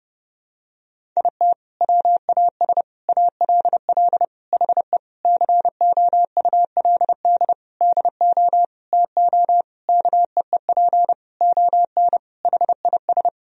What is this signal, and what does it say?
Morse code, heard as a keyed tone, about 30 wpm, 700 Hz. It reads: IT WAS ALL HE COULD DO TO KEEP ON HIS
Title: Sentences from Top 1000 Words - Review Part 1 - 30wpm